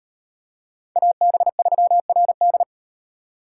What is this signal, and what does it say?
Morse code, heard as a keyed tone, about 37 wpm, 700 Hz. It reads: AB3RD